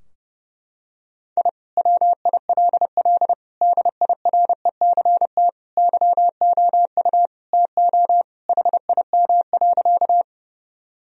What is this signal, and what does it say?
Morse code, heard as a keyed tone, about 30 wpm, 700 Hz. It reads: I WILL DIRECT YOU TO HIM.
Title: Sentences from Top 1000 Words - Review Part 2 - 30wpm